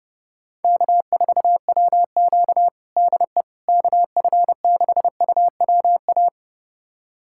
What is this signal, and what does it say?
Morse code, heard as a keyed tone, about 30 wpm, 690 Hz. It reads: K4WQ DE KF6UWA
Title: USA Callsign DE Callsign 30wpm